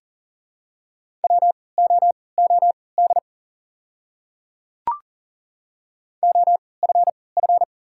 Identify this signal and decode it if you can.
Morse code, heard as a keyed tone, about 38 wpm, spaced as for 22 wpm, 700 Hz. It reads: WOOD OFF
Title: 3x Character Spacing - Top 300 Words - 40wpm